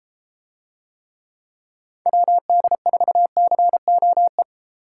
Morse code, heard as a keyed tone, about 31 wpm, 700 Hz. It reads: WD4COE